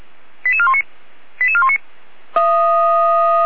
Five-tone selective call (ZVEI), heard as quick runs of five tones, tones of about 70 ms. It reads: 88419 88419